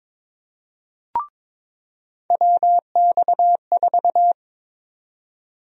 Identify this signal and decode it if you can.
Morse code, heard as a keyed tone, about 22 wpm, 700 Hz. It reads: WX4
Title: USA Call Sign Prefixes - 22wpm